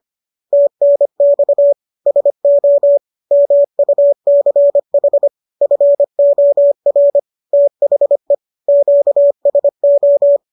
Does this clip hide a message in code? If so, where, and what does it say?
Yes: Morse code from 0.5 to 10.4 s, heard as a keyed tone, about 25 wpm, 570 Hz. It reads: TNX SO MUCH FOR THE QSO